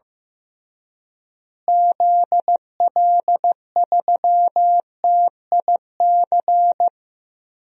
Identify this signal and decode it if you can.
Morse code, heard as a keyed tone, about 15 wpm, 700 Hz. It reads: ZL3TIC
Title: DXCC-100 Call Signs - 6 Characters - Repeated 2x - 15wpm